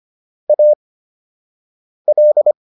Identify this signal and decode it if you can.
Morse code, heard as a keyed tone, about 25 wpm, spaced as for 6 wpm, 600 Hz. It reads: AL